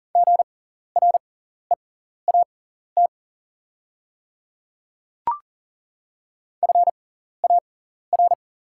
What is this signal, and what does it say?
Morse code, heard as a keyed tone, about 39 wpm, spaced as for 14 wpm, 690 Hz. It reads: GREAT FAR
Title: 6x Character Spacing - Top 500 Words - 40wpm